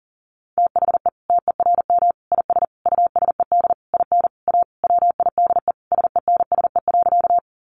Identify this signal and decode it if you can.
Morse code, heard as a keyed tone, about 40 wpm, 700 Hz. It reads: THE TERM IS USED IN A WIDE SENSE.
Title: Sentences from Top 900 Words 40wpm